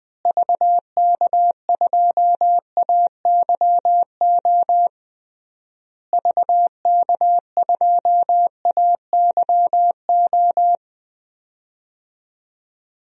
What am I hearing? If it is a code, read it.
Morse code, heard as a keyed tone, about 21 wpm, 700 Hz. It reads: VK2AYO VK2AYO
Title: DXCC-100 Call Signs - 6 Characters - Repeated 3x - 20wpm